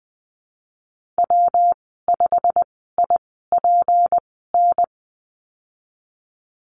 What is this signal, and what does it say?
Morse code, heard as a keyed tone, about 20 wpm, spaced as for 15 wpm, 700 Hz. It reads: W5IPN
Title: 154 Lesson 25g - Letter p - Call Signs 20wpm